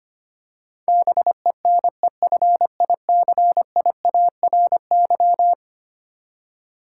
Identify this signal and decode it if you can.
Morse code, heard as a keyed tone, about 25 wpm, 700 Hz. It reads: BENEFICIARY
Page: morse-code-ninja-practice-25wpm